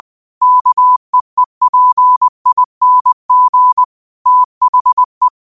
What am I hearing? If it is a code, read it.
Morse code, heard as a keyed tone, about 20 wpm, 1000 Hz. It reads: KEEPING THE
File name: qotdfiles / 20wpm